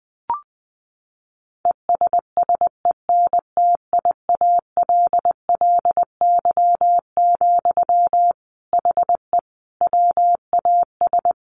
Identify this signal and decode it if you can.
Morse code, heard as a keyed tone, about 20 wpm, 700 Hz. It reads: ESSENTIALLY, HE WAS
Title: Sentences from Top 1300 Words 20wpm